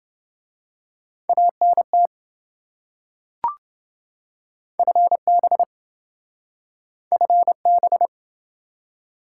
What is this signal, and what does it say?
Morse code, heard as a keyed tone, about 27 wpm, 700 Hz. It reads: ANT FB FB